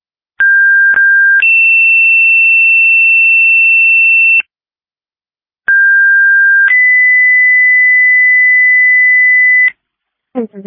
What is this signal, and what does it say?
Two-tone page, heard as a one-second tone, then a three-second tone, about 1600 then 2700 Hz, about 1600 then 2000 Hz, with speech. It is heard from 0.4 to 9.7 s.